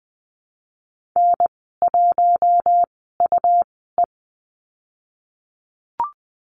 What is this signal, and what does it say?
Morse code, heard as a keyed tone, about 20 wpm, spaced as for 14 wpm, 700 Hz. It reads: N1UE
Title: 167 Lesson 27g - Number 7 - Call Signs 20wpm